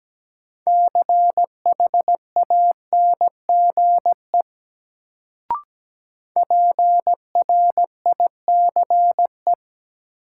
Morse code, heard as a keyed tone, about 17 wpm, 700 Hz. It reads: CHANGE PRICE
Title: Sentences from Top 600 Words - New Word Drill - 17wpm